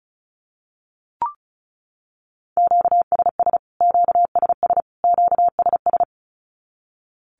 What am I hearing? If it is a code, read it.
Morse code, heard as a keyed tone, about 34 wpm, 700 Hz. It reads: QSS QSS QSS
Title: Q-Codes Repeated 3x 35wpm